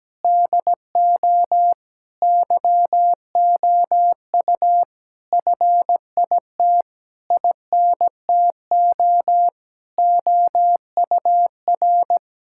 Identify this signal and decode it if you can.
Morse code, heard as a keyed tone, about 17 wpm, 700 Hz. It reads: DO YOU FIT INTO OUR
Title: Sentences from Top 1000 Words - Review Part 6 - 17wpm